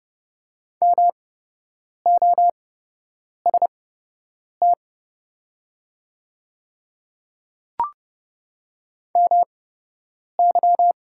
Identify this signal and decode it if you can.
Morse code, heard as a keyed tone, about 30 wpm, spaced as for 8 wpm, 700 Hz. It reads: MOST MY
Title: 8x Character Spacing - Top 100 Words - 30wpm